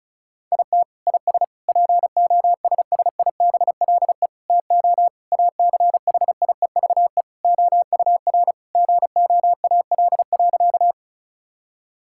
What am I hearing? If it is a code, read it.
Morse code, heard as a keyed tone, about 34 wpm, 700 Hz. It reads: IT IS POSSIBLE TO ACHIEVE OUR GOAL.
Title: Sentences from Top 1000 Words - Review Part 6 - 35wpm